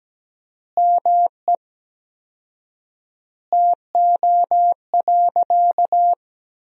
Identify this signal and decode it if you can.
Morse code, heard as a keyed tone, about 17 wpm, 700 Hz. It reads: ME TO.
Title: 4x Word-Spacing - Sentences from Top 400 Words - 17wpm